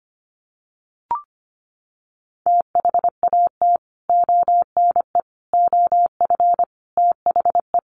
Morse code, heard as a keyed tone, about 25 wpm, 700 Hz. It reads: THAT ONE OF THE
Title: Sets of 4 Words 25wpm